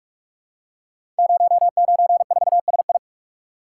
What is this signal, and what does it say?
Morse code, heard as a keyed tone, about 44 wpm, 700 Hz. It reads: 094SI
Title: Code Groups - 5 Letter:Numbers - 45wpm